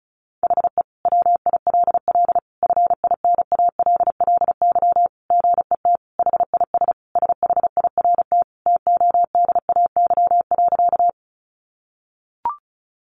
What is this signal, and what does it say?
Morse code, heard as a keyed tone, about 34 wpm, 700 Hz. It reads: HE WILL FINALLY GET HIS SHIRT TODAY.
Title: Additional Easy Phrases 35wpm